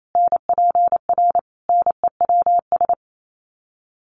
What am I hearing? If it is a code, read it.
Morse code, heard as a keyed tone, about 28 wpm, 700 Hz. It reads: NPR NEWS
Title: Sets of 5 Words Encore 28wpm